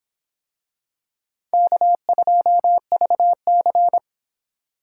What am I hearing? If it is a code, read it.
Morse code, heard as a keyed tone, about 26 wpm, 700 Hz. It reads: K2VC